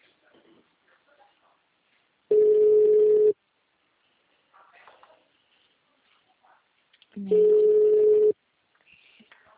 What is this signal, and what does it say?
Ringback tone (Europe, 425 Hz).